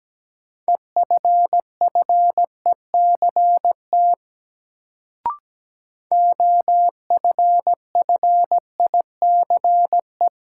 Morse code, heard as a keyed tone, about 17 wpm, 700 Hz. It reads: EFFECT OFFICE
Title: Sentences from Top 1000 Words - New Word Drill - 17wpm